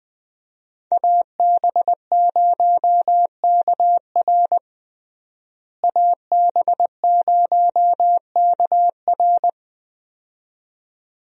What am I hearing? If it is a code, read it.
Morse code, heard as a keyed tone, about 20 wpm, 700 Hz. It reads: AB0KR AB0KR
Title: USA Call Signs - 5 Characters - Repeated 2x - 20wpm